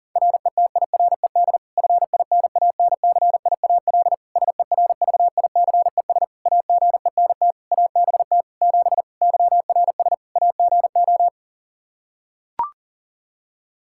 Morse code, heard as a keyed tone, about 37 wpm, 700 Hz. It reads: RETIRED FINANCIAL SERVICES AGENT ABT 7 YRS AGO
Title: K1USN SST QSO Practice 40wpm